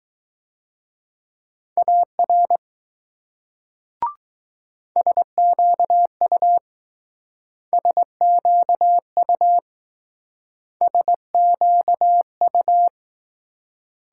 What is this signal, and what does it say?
Morse code, heard as a keyed tone, about 20 wpm, 700 Hz. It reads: AR SQU SQU SQU